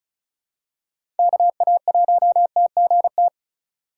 Morse code, heard as a keyed tone, about 34 wpm, 700 Hz. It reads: KA1TGT
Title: USA Call Signs - 6 Characters - 3x Character Spacing - 35wpm